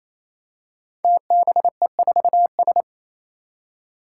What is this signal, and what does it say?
Morse code, heard as a keyed tone, about 27 wpm, 700 Hz. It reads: TBE4S
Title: Five Challenging Letter-Number Combinations 28wpm